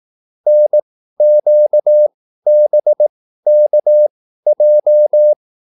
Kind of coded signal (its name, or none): Morse code